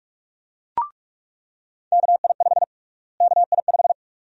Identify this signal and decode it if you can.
Morse code, heard as a keyed tone, about 43 wpm, 700 Hz. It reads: KI5 KI5
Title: USA Call Sign Prefixes - Repeated 2x - 45wpm